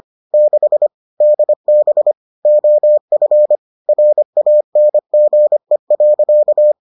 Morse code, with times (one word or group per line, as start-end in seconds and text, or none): 0.3-0.9 6
1.2-2.1 DB
2.5-3.6 OF
3.9-6.7 RANGE.